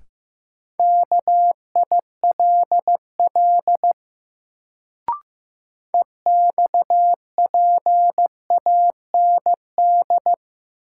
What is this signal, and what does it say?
Morse code, heard as a keyed tone, about 15 wpm, 700 Hz. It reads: KILL EXPAND
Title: Sentences from Top 1400 Words - New Word Drill Alt - 15wpm